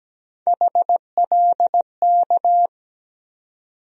Morse code, heard as a keyed tone, about 17 wpm, 700 Hz. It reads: HLK
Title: USA Call Sign Postfixes - 3 Letters 17wpm